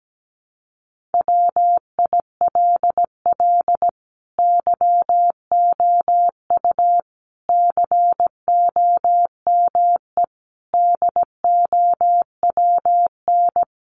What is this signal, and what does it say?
Morse code, heard as a keyed tone, about 17 wpm, 700 Hz. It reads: WILL YOU COME DOWN
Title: Sentences from Top 1500 Words 17wpm